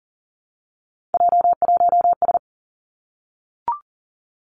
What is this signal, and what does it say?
Morse code, heard as a keyed tone, about 42 wpm, 700 Hz. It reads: J1S
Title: Three Challenging Letter-Number Combinations 40wpm